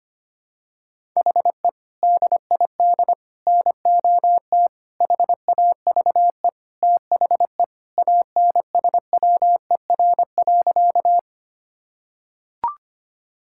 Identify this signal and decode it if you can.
Morse code, heard as a keyed tone, about 25 wpm, 700 Hz. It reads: HE DID NOT HAVE THE ANSWER.
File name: practice / 25wpm / Sentences with 4-Word Phrases - 25wpm